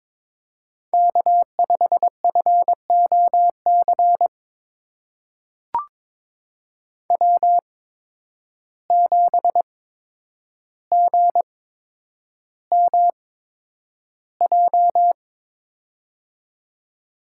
Morse code, heard as a keyed tone, about 22 wpm, spaced as for 9 wpm, 700 Hz. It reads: K5FOC W7GMJ
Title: USA Call Signs - 5 Characters - 8x Character Spacing - 22wpm